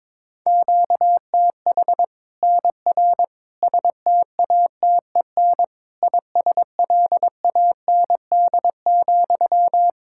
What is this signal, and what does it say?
Morse code, heard as a keyed tone, about 22 wpm, 700 Hz. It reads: QTH NR STATEN ISLAND,
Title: Practice QSO - Exchange 1 - Beginning - 22wpm